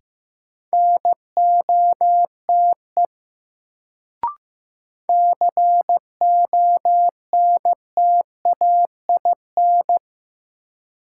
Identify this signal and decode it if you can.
Morse code, heard as a keyed tone, about 15 wpm, 700 Hz. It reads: NOTE CONTAIN
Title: Sentences from Top 500 Words - New Word Drill - 15wpm